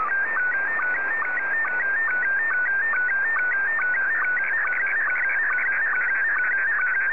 SSTV (slow-scan television), Scottie 1.